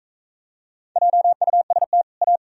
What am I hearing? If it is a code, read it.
Morse code, heard as a keyed tone, about 40 wpm, 700 Hz. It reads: JUST A